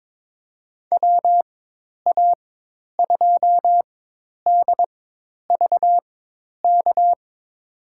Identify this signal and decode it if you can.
Morse code, heard as a keyed tone, about 22 wpm, spaced as for 10 wpm, 700 Hz. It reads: WA2DVK
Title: USA Call Signs - 6 Characters - 4x Character Spacing - 22wpm